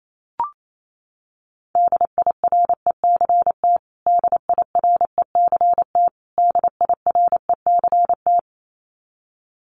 Morse code, heard as a keyed tone, about 28 wpm, 700 Hz. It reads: DIRECT DIRECT DIRECT
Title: Top 500 Words Repeated 28wpm